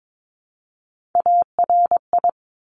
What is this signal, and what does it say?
Morse code, heard as a keyed tone, about 22 wpm, 700 Hz. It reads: ARI
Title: Top Three Letter Combinations in Words 22wpm